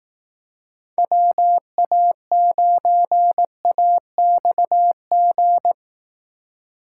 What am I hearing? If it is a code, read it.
Morse code, heard as a keyed tone, about 18 wpm, 700 Hz. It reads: WA9AXG
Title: Speed Racing - USA Call Signs 15wpm